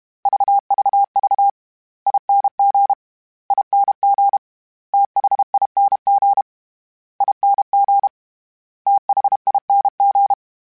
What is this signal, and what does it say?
Morse code, heard as a keyed tone, about 32 wpm, 800 Hz. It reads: VVV ING ING THING ING THING
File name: Special INGs / Special INGs-mp3s / SlowToFaster ing @30